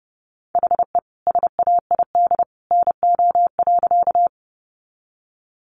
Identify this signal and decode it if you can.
Morse code, heard as a keyed tone, about 30 wpm, 700 Hz. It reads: HE SAID NO.
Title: Sentences from Top 100 Words 30wpm